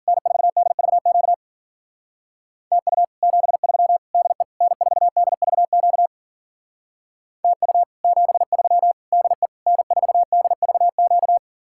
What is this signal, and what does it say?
Morse code, heard as a keyed tone, about 52 wpm, 700 Hz. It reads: N4DVQ TU 73 DE N4DVQ TU 73 DE N4DVQ